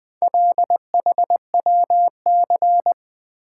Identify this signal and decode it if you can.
Morse code, heard as a keyed tone, about 20 wpm, 700 Hz. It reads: LHWC
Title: Four Letter-Number Combinations - 20wpm